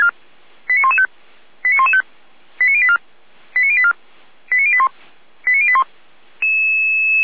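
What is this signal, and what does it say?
Five-tone selective call (ZVEI), heard as quick runs of five tones, tones of about 70 ms. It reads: 89115 89115 89084 89084 89081 89081
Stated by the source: Five-tone ZVEI selective call received on an SDR.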